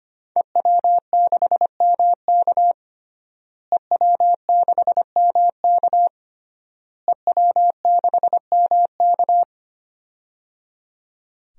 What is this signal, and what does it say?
Morse code, heard as a keyed tone, about 25 wpm, 700 Hz. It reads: EW6MK EW6MK EW6MK
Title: DXCC-100 Call Signs - 5 Characters - Repeated 3x - 25wpm